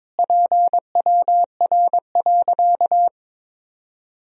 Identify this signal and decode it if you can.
Morse code, heard as a keyed tone, about 22 wpm, 700 Hz. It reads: PWR.
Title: Sentences from Top 500 Words and Abbreviations 22wpm